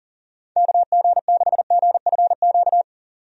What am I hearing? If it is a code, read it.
Morse code, heard as a keyed tone, about 40 wpm, 700 Hz. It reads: KG6GFQ
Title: USA Call Signs - 6 Characters - 40wpm